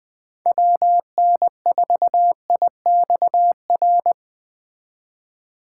Morse code, heard as a keyed tone, about 20 wpm, 700 Hz. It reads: WN4IXR
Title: Speed Racing - USA Call Signs 17wpm